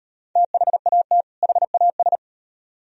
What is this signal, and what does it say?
Morse code, heard as a keyed tone, about 38 wpm, 700 Hz. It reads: THAT HAS